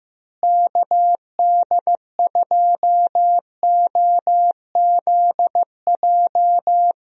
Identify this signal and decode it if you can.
Morse code, heard as a keyed tone, about 15 wpm, 700 Hz. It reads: KD2OZJ
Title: USA Call Signs - 6 Characters - Repeated 2x - 15wpm